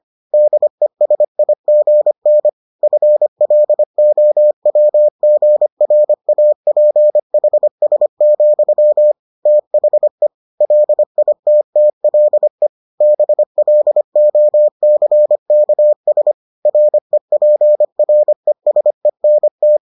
Morse code, heard as a keyed tone, about 25 wpm, 600 Hz. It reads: DESIGN FLOWGRAPHS, THE LITTLE BLOCKS REPRESENT